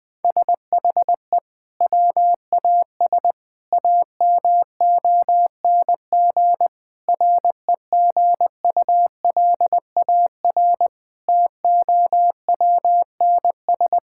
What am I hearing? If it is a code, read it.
Morse code, heard as a keyed tone, about 20 wpm, 700 Hz. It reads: SHE WAS AMONG REGULAR TOWNS
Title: Sentences from Top 1000 Words - Review Part 6 - 20wpm